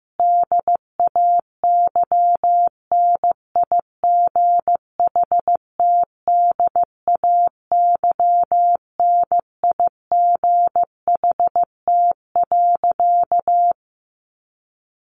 Morse code, heard as a keyed tone, about 15 wpm, 700 Hz. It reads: DAYNIGHTDAYNIGHT.
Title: A Sound Of Thunder 15wpm